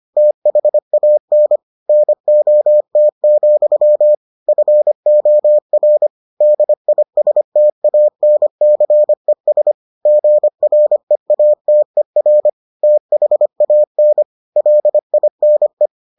Morse code, heard as a keyed tone, about 25 wpm, 600 Hz. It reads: THAN NOT, FOR DISTANCES GREATER THAN LINE